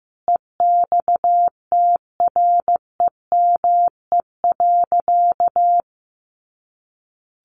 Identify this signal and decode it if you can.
Morse code, heard as a keyed tone, about 15 wpm, 700 Hz. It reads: EXTREME.
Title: Sentences from Top 1500 Words 15wpm